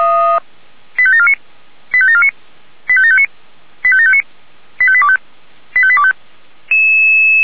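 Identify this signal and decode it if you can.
Five-tone selective call (ZVEI), heard as quick runs of five tones, tones of about 80 ms. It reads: 85749 85749 85759 85759 85825 85825
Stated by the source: ZVEI five-tone call 85730 received by SDR.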